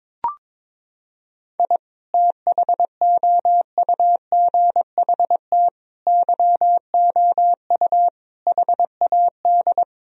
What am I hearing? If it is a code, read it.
Morse code, heard as a keyed tone, about 22 wpm, 700 Hz. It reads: I THOUGHT YOU HAD
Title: Sentences from Top 1000 Words - Review Part 3 - 22wpm